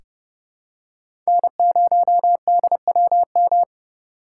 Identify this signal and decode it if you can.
Morse code, heard as a keyed tone, about 30 wpm, 700 Hz. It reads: N0DWM
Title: USA Call Signs - 5 Characters - 4x Character Spacing - 30wpm